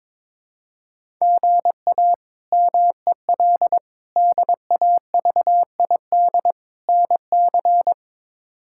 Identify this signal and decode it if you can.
Morse code, heard as a keyed tone, about 22 wpm, 700 Hz. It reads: GA MEL DAVID NC
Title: K1USN SST QSO Practice - 22wpm